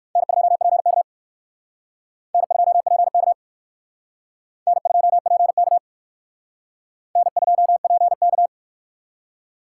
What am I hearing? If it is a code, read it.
Morse code, heard as a keyed tone, about 56 wpm, 700 Hz. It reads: N2PK N2PK N2PK N2PK
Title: Speed Racing - USA 1x2 Call Signs 45wpm